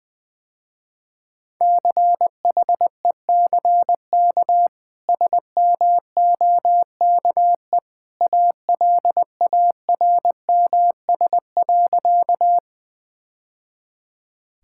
Morse code, heard as a keyed tone, about 20 wpm, 700 Hz. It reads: CHECK SMOKE ALARMS.
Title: Easy Sentences with Hints - 20wpm